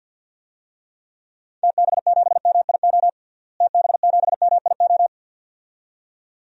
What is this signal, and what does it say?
Morse code, heard as a keyed tone, about 52 wpm, 700 Hz. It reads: TB7MIO TB7MIO
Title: DXCC-100 Call Signs - 6 Characters - Repeated 2x - 50wpm